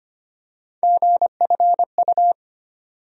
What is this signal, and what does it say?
Morse code, heard as a keyed tone, about 25 wpm, 700 Hz. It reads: GFU
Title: USA Call Sign Postfixes - 3 Letters 25wpm